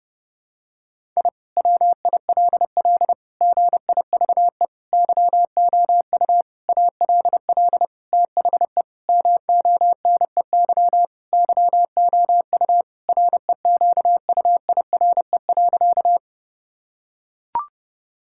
Morse code, heard as a keyed tone, about 30 wpm, 700 Hz. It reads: I WILL GIVE YOU ALL THE MONEY YOU REQUIRE.
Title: Sentences from Top 1000 Words - Review Part 6 - 30wpm